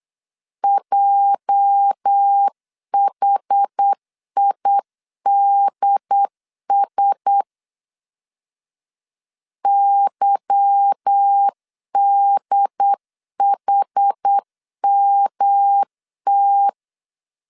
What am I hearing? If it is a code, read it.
Morse code, heard as a keyed tone, about 8 wpm, 800 Hz. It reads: JHIDS YDHMT
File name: LEZIONI / 9 286 8wpm 3425